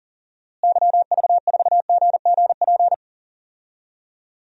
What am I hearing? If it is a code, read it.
Morse code, heard as a keyed tone, about 40 wpm, 700 Hz. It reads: YV4GGP
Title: DXCC-100 Call Signs - 6 Characters - Repeated 2x - 40wpm